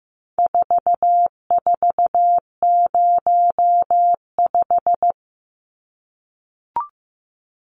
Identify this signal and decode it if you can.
Morse code, heard as a keyed tone, about 15 wpm, 700 Hz. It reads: 4405
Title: Numbers - 4-Digit 15wpm